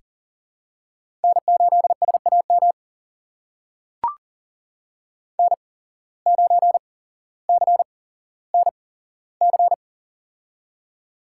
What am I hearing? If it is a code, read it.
Morse code, heard as a keyed tone, about 40 wpm, spaced as for 17 wpm, 700 Hz. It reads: N8SAM N9CNC
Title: USA Call Signs - 5 Characters - 8x Character Spacing - 40wpm